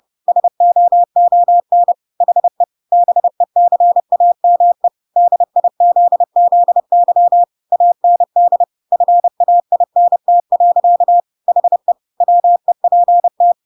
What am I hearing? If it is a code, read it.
Morse code, heard as a keyed tone, about 30 wpm, 700 Hz. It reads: SOON HE BECAME DIZZY AND FAINT. HE WEPT
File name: Pinocchio / Pinocchio ch5 30 30 ik1hge